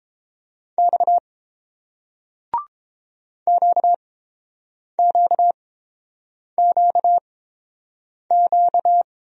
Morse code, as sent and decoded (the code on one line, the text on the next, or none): -..- / --.- --.- --.- --.-
X QQQQ